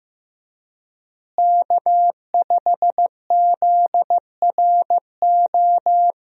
Morse code, heard as a keyed tone, about 15 wpm, 700 Hz. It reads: K5ZRO
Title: USA Call Signs - 5 Characters - Repeated 3x - 15wpm